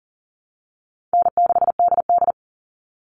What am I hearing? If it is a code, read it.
Morse code, heard as a keyed tone, about 40 wpm, 700 Hz. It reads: N6DD
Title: All Cut Numbers - 4-Digit 40wpm